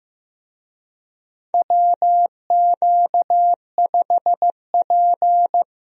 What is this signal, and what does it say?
Morse code, heard as a keyed tone, about 15 wpm, 700 Hz. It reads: WQ5P
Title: USA Call Signs - 4 Characters - 15wpm